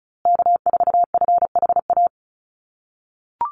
Morse code, heard as a keyed tone, about 34 wpm, 700 Hz. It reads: K4FHA
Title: USA Call Signs Repeated 2x 35wpm